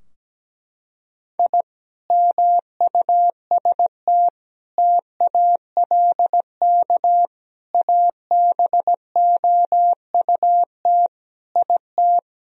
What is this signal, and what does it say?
Morse code, heard as a keyed tone, about 17 wpm, 700 Hz. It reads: I MUST TALK ABOUT IT
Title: Sentences from Top 1000 Words - Review Part 4 - 17wpm